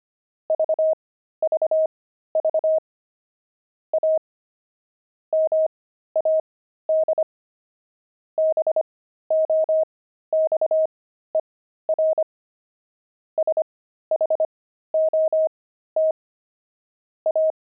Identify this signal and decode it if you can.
Morse code, heard as a keyed tone, about 25 wpm, spaced as for 13 wpm, 630 Hz. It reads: VVV A MAD BOXER SHOT A